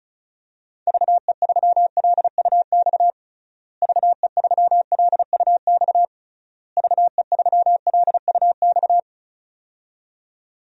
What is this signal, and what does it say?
Morse code, heard as a keyed tone, about 36 wpm, 700 Hz. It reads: VE3LUX VE3LUX VE3LUX
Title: DXCC-100 Call Signs - 6 Characters - Repeated 3x - 35wpm